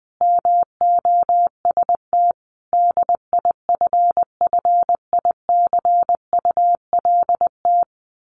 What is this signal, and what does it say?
Morse code, heard as a keyed tone, about 20 wpm, 700 Hz. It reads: MOST DIFFICULT
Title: Sets of 5 Words Encore2 20wpm